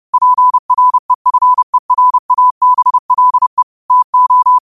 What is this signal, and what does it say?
Morse code, heard as a keyed tone, about 30 wpm, 1000 Hz. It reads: PREFERABLE TO
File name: qotdfiles / 30wpm